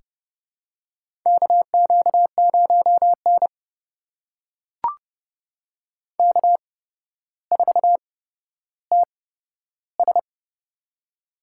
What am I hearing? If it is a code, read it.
Morse code, heard as a keyed tone, about 30 wpm, spaced as for 13 wpm, 700 Hz. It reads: KQ0N K4TS
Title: USA Call Signs - 4 Characters - 8x Character Spacing - 30wpm